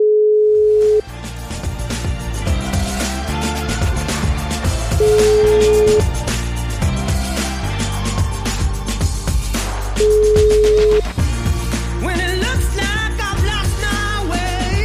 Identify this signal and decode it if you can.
Ringback tone (Europe, 425 Hz), with noise.